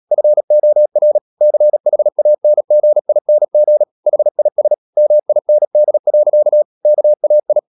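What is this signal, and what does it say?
Morse code, heard as a keyed tone, about 37 wpm, 600 Hz. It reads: FOR CHANGING HIS MIND. KAI
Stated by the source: Morse code practice at 35–39 wpm sending English text of Ken Follett's novel Never, from CW med Gnister.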